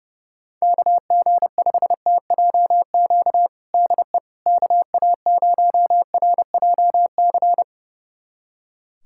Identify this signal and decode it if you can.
Morse code, heard as a keyed tone, about 30 wpm, 700 Hz. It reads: KG5TJQ DE KA0RJC
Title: USA Callsign DE Callsign - 6 Characters - 30wpm